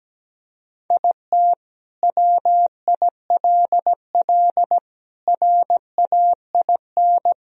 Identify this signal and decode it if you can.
Morse code, heard as a keyed tone, about 17 wpm, 700 Hz. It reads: IT WILL RAIN
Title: Sentences from Top 1000 Words - Review Part 1 - 17wpm